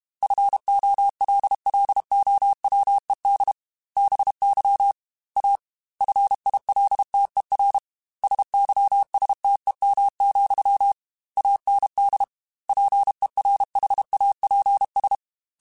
Morse code, heard as a keyed tone, about 32 wpm, 800 Hz. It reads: FOLLOWED BY A FILTER SYSTEM, AND PERHAPS
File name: Technical Subjects / Technical Subjects-mp3s / Sw Pwr Sup @30